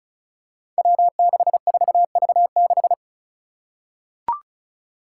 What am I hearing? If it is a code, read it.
Morse code, heard as a keyed tone, about 36 wpm, 700 Hz. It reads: W64V6
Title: Five Challenging Letter-Number Combinations 35wpm